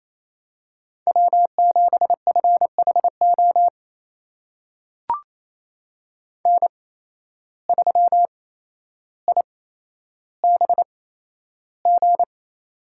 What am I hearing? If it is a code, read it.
Morse code, heard as a keyed tone, about 28 wpm, spaced as for 12 wpm, 700 Hz. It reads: W7FHO N3IBG
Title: USA Call Signs - 5 Characters - 8x Character Spacing - 28wpm